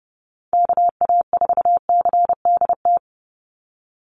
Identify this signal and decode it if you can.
Morse code, heard as a keyed tone, about 30 wpm, 700 Hz. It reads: KA4CDT
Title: USA Call Signs 30wpm